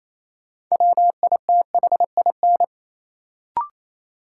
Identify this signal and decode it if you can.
Morse code, heard as a keyed tone, about 28 wpm, 700 Hz. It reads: WITHIN
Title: Sentences from Top 700 Words - New Word Drill - 28wpm